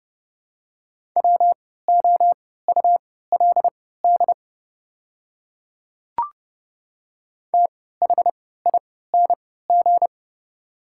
Morse code, heard as a keyed tone, about 30 wpm, spaced as for 17 wpm, 690 Hz. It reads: WOULD THING
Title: 3x Character Spacing - Top 100 Words - 30wpm